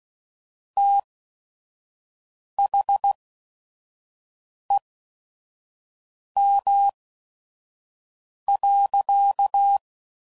Morse code, heard as a keyed tone, about 16 wpm, spaced as for 5 wpm, 800 Hz. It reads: THEM.
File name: CW Code Groups @5WPM